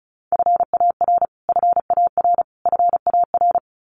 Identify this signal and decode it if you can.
Morse code, heard as a keyed tone, about 34 wpm, 700 Hz. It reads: FAR FAR FAR
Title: Top 500 Words Repeated 35wpm